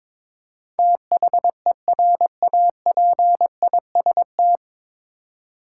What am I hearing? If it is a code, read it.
Morse code, heard as a keyed tone, about 22 wpm, 700 Hz. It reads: THERAPIST
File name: 9 Letter Words - First 3 Letters Given - 22wpm